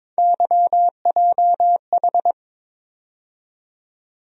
Morse code, heard as a keyed tone, about 22 wpm, 700 Hz. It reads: YJH
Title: USA Call Sign Postfixes - 3 Letters 22wpm